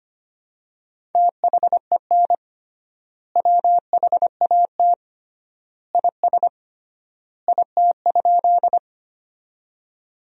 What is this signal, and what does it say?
Morse code, heard as a keyed tone, about 25 wpm, 700 Hz. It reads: THEN WHAT IS IT?
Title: 3x Word-Spacing - Sentences from Top 100 Words - 25wpm